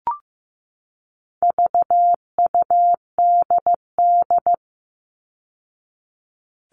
Morse code, heard as a keyed tone, about 15 wpm, 700 Hz. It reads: VUDD
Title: All Cut Numbers - 4-Digit 15wpm